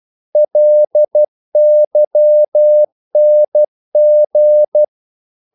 Morse code, heard as a keyed tone, about 12 wpm, 600 Hz. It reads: LYNG
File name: Korte ord DK 12wpm